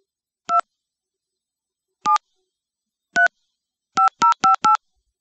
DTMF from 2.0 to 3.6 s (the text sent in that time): 7 3